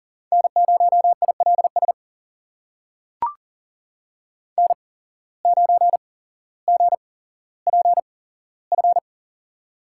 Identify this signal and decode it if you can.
Morse code, heard as a keyed tone, about 39 wpm, spaced as for 17 wpm, 700 Hz. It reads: N0ILS N9GPF
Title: USA Call Signs - 5 Characters - 8x Character Spacing - 40wpm